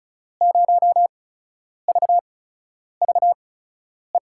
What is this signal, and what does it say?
Morse code, heard as a keyed tone, about 36 wpm, spaced as for 10 wpm, 700 Hz. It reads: 0VVE
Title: USA Call Signs - 8x Character Spacing - 35wpm